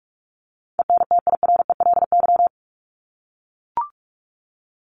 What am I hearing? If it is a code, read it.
Morse code, heard as a keyed tone, about 44 wpm, 700 Hz. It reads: ENTIRELY
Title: LY Suffix - Top 100 Words with Base Words 45wpm